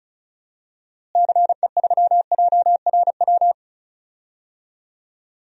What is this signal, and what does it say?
Morse code, heard as a keyed tone, about 36 wpm, 700 Hz. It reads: CE3JRW
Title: DXCC-100 Call Signs - 6 Characters - Repeated 2x - 35wpm